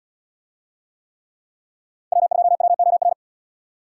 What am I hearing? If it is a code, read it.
Morse code, heard as a keyed tone, about 77 wpm, 700 Hz. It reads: W2WOR